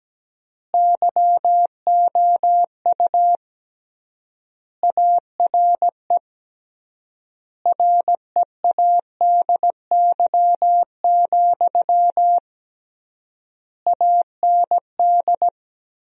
Morse code, heard as a keyed tone, about 17 wpm, 700 Hz. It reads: YOU ARE READY, AND
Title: 3x Word-Spacing - Sentences from Top 500 Words - 17wpm